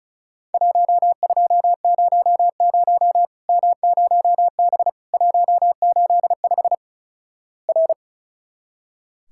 Morse code, heard as a keyed tone, about 36 wpm, 700 Hz. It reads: 1200 M0B 185 R
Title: Worked All Europe DX Contest - QTC Exchange - 35wpm